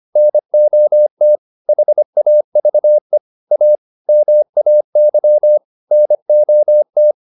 Morse code, heard as a keyed tone, about 25 wpm, 600 Hz. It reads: NOT HAVE A MAY NOT